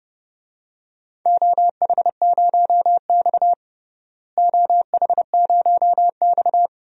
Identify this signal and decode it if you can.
Morse code, heard as a keyed tone, about 30 wpm, 700 Hz. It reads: OH0X OH0X
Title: DXCC-100 Call Signs - 4 Characters - Repeated 3x - 30wpm